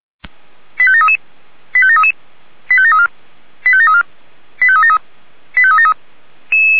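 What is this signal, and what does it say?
Five-tone selective call (ZVEI), heard as quick runs of five tones, tones of about 70 ms. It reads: 85733 85733 85734 85734 85373 85373
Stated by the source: ZVEI five-tone selective call received on an SDR.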